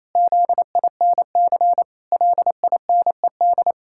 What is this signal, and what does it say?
Morse code, heard as a keyed tone, about 28 wpm, 700 Hz. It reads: ZINC LINED
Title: Pangram Challenge v2 - 28wpm